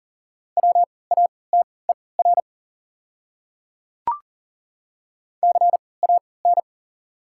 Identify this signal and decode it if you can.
Morse code, heard as a keyed tone, about 40 wpm, spaced as for 23 wpm, 690 Hz. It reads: WATER CAN
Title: 3x Character Spacing - Top 300 Words - 40wpm